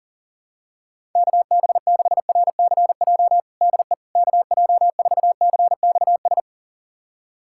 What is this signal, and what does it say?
Morse code, heard as a keyed tone, about 40 wpm, 700 Hz. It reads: KB6RCJ DE KJ4CXS
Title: USA Callsign DE Callsign - 6 Characters - 40wpm